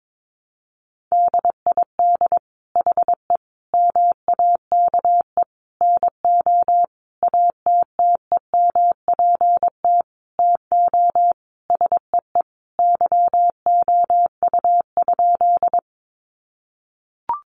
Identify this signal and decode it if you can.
Morse code, heard as a keyed tone, about 22 wpm, 700 Hz. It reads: DID HE MAKE NO ATTEMPT TO SEE YOU?
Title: Sentences from Top 1000 Words 22wpm